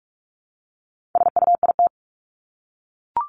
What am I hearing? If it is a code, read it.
Morse code, heard as a keyed tone, about 44 wpm, 700 Hz. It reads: SUIT
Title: Top 1501-1750 Words 45wpm